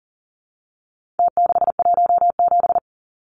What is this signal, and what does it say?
Morse code, heard as a keyed tone, about 38 wpm, 700 Hz. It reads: T617
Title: Numbers - Contest Style - 4-Digit 40wpm